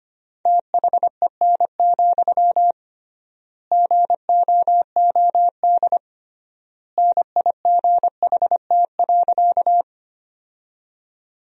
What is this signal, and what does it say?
Morse code, heard as a keyed tone, about 25 wpm, 700 Hz. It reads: THEN, GOOD NIGHT.
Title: 3x Word-Spacing - Sentences from Top 300 Words - 25wpm